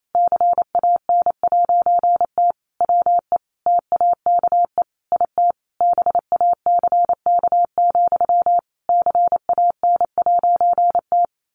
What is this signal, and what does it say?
Morse code, heard as a keyed tone, about 28 wpm, 700 Hz. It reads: CAN'T WE TAKE IT BACK, CAN'T
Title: A Sound Of Thunder 28wpm